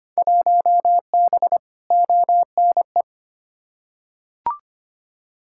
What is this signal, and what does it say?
Morse code, heard as a keyed tone, about 25 wpm, 700 Hz. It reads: 1B ONE
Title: Field Day Exchanges - 25wpm